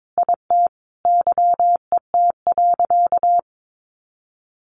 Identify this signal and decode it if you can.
Morse code, heard as a keyed tone, about 22 wpm, 700 Hz. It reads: IT YET.
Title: Sentences from Top 1100 Words 22wpm